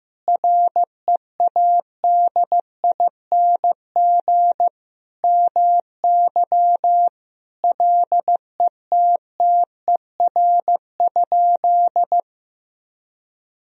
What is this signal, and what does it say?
Morse code, heard as a keyed tone, about 15 wpm, 700 Hz. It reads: READING MY LETTER?
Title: Sentences from Top 1000 Words - Review Part 3 - 15wpm